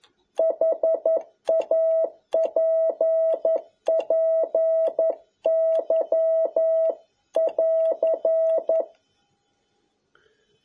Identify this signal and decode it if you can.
Morse code, heard as a keyed tone, about 11 wpm, 650 Hz. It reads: HAPPY <AR>
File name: jh8cbh / 2021cw